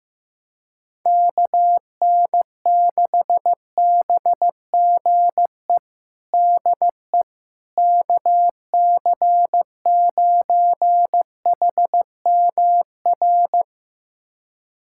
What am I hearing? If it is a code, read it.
Morse code, heard as a keyed tone, about 15 wpm, 700 Hz. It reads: KN6BGE DE KC9HMR
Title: USA Callsign DE Callsign 15wpm